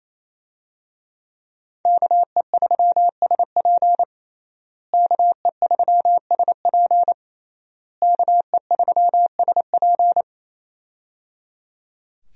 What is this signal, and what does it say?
Morse code, heard as a keyed tone, about 28 wpm, 700 Hz. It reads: KE3SP KE3SP KE3SP
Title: USA Call Signs - 5 Characters - Repeated 3x - 28wpm